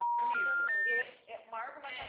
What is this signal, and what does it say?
Special information tone, with speech.
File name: out- -2007- - - .19585